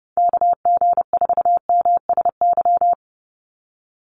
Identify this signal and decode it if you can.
Morse code, heard as a keyed tone, about 30 wpm, 700 Hz. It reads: KG4MSY
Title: USA Call Signs 30wpm